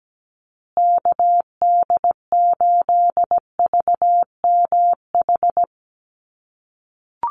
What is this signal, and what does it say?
Morse code, heard as a keyed tone, about 17 wpm, 700 Hz. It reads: KD8VMH
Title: USA Call Signs Repeated 3x 17wpm